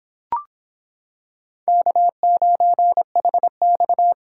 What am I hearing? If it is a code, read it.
Morse code, heard as a keyed tone, about 25 wpm, 700 Hz. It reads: K9HX